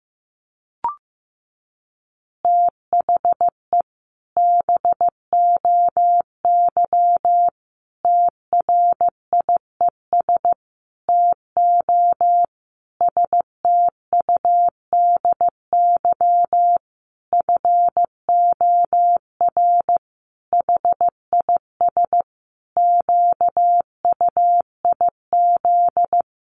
Morse code, heard as a keyed tone, about 15 wpm, 700 Hz. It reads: THE BOY TRIES TO STUDY FOR HIS QUIZ
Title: Additional Easy Phrases 15wpm